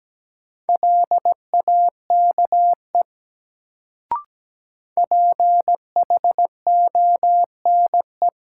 Morse code, heard as a keyed tone, about 17 wpm, 700 Hz. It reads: LAKE PHONE
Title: Sentences from Top 1100 Words - New Word Drill Alt - 17wpm